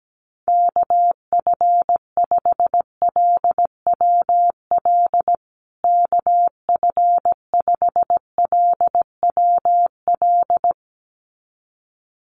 Morse code, heard as a keyed tone, about 17 wpm, 700 Hz. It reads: KF5LWL KF5LWL
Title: USA Call Signs Repeated 2x 17wpm